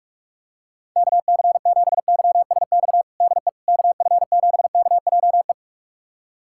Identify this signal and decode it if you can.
Morse code, heard as a keyed tone, about 41 wpm, 700 Hz. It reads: KC7YSX DE KF7KJE
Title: USA Callsign DE Callsign - 6 Characters - 45wpm